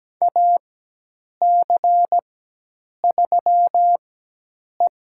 Morse code, heard as a keyed tone, about 17 wpm, spaced as for 8 wpm, 700 Hz. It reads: AC3E
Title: USA Call Signs - 4 Characters - 4x Character Spacing - 17wpm